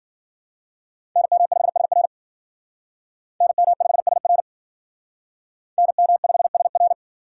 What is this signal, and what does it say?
Morse code, heard as a keyed tone, about 53 wpm, 700 Hz. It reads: NM5SR NM5SR NM5SR